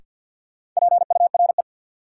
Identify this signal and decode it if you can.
Morse code, heard as a keyed tone, about 49 wpm, 700 Hz. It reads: PURE
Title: Sentences from Top 1500 Words - New Word Drill - 50wpm